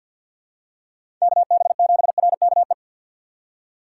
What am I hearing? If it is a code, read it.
Morse code, heard as a keyed tone, about 53 wpm, 700 Hz. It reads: KB7RKE
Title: USA Call Signs - 6 Characters - 6x Character Spacing - 50wpm